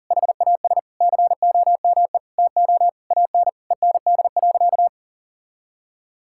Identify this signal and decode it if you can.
Morse code, heard as a keyed tone, about 37 wpm, 700 Hz. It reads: HAS COME TO AN END.
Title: Sentences from Top 1000 Words - Review Part 5 - 40wpm